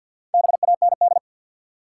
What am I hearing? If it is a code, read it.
Morse code, heard as a keyed tone, about 47 wpm, 700 Hz. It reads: BAND
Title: Four Letter Words - 50wpm